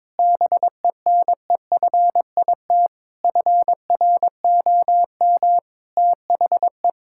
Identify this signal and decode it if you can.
Morse code, heard as a keyed tone, about 22 wpm, 700 Hz. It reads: BENEFIT FROM THE
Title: Sentences from Top 1000 Words - Review Part 6 - 22wpm